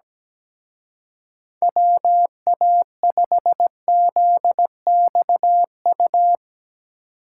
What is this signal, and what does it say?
Morse code, heard as a keyed tone, about 18 wpm, 700 Hz. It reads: WA5ZXU
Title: Speed Racing - USA 2x3 Call Signs 17wpm